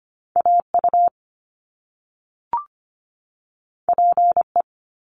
Morse code, heard as a keyed tone, about 25 wpm, 700 Hz. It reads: AU PE
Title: Top Two Letter Combinations in Words 25wpm